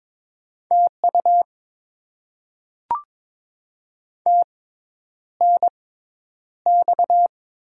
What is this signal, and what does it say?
Morse code, heard as a keyed tone, about 22 wpm, spaced as for 10 wpm, 700 Hz. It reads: TU TNX
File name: Top 20 QSO Elements - 6x Character Spacing - 22wpm